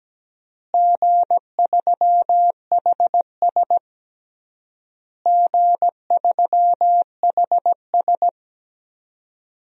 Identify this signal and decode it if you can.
Morse code, heard as a keyed tone, about 17 wpm, 700 Hz. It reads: G3HS G3HS
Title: DXCC-100 Call Signs - 4 Characters - Repeated 3x - 17wpm